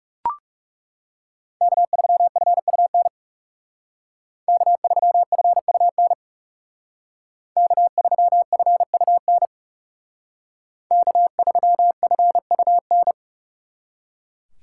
Morse code, heard as a keyed tone, about 38 wpm, 700 Hz. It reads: K3FUN K3FUN K3FUN K3FUN